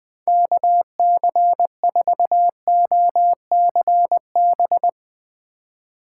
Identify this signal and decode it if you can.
Morse code, heard as a keyed tone, about 20 wpm, 700 Hz. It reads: KC4OCB
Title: USA Call Signs - 6 Characters - 8x Character Spacing - 20wpm